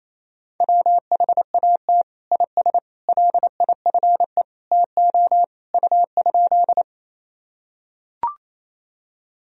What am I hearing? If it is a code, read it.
Morse code, heard as a keyed tone, about 28 wpm, 700 Hz. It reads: WHAT IS LIFE TO U?
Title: Sentences from Top 300 Words and Abbreviations 28wpm